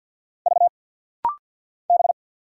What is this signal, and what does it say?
Morse code, heard as a keyed tone, about 49 wpm, 700 Hz. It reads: V B
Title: Single Letters - Rapid-Fire 50wpm